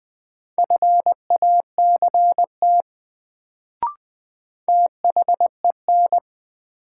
Morse code, heard as a keyed tone, about 20 wpm, 700 Hz. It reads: FACT THEN
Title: Sets of 2 Words - New Word Drill - 20wpm